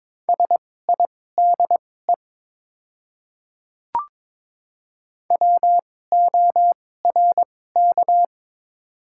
Morse code, heard as a keyed tone, about 21 wpm, spaced as for 16 wpm, 700 Hz. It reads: SIDE WORK
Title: 2x Character Spacing - Top 300 Words - 22wpm